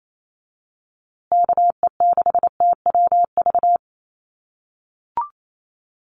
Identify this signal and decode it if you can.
Morse code, heard as a keyed tone, about 28 wpm, 700 Hz. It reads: KE6TWV
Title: USA Call Signs Repeated 2x 28wpm